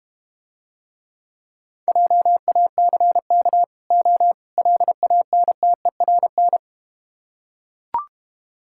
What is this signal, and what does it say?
Morse code, heard as a keyed tone, about 31 wpm, 700 Hz. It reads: JACK O LANTERN
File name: Easy Words - Speed-Racing with Hints - 25wpm